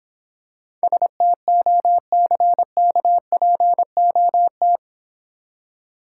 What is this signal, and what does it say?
Morse code, heard as a keyed tone, about 24 wpm, 700 Hz. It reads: STOCKPOT
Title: Easy Words - Speed-Racing with Hints - 20wpm